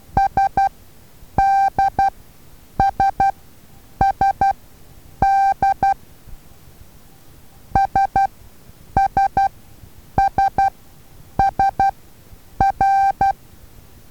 Morse code, heard as a keyed tone, about 12 wpm, spaced as for 8 wpm, 780 Hz. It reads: SDSSD SSSSR
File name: Taviro HA6VA A